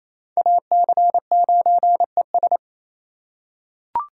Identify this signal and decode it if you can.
Morse code, heard as a keyed tone, about 28 wpm, 700 Hz. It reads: AC9ES
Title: USA Call Signs - 2x Character Spacing - 28wpm